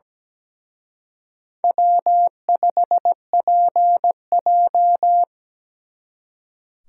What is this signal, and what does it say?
Morse code, heard as a keyed tone, about 20 wpm, 700 Hz. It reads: W5PJ
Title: Speed Racing - USA 1x2 Call Signs 17wpm